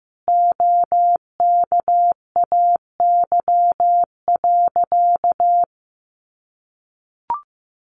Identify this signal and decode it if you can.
Morse code, heard as a keyed tone, about 14 wpm, 700 Hz. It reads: OKAY.
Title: A Sound Of Thunder 15wpm